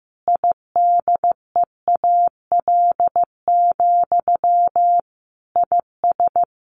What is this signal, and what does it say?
Morse code, heard as a keyed tone, about 15 wpm, 700 Hz. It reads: IDEAL, IS
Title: Sentences from Top 1400 Words 15wpm